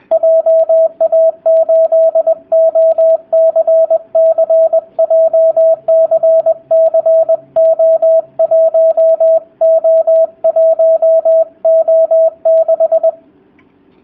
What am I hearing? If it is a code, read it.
Morse code, heard as a keyed tone, about 20 wpm, 650 Hz, with radio noise. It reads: JA8OCC JCCO1O1O6
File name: jh8cbh / 2019cw / sp20- w